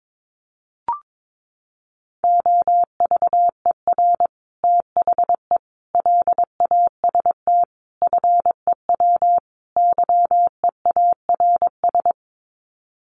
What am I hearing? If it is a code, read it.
Morse code, heard as a keyed tone, about 22 wpm, 700 Hz. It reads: OVER THE LAST FEW YEARS
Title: Sets of 5 Words Encore 22wpm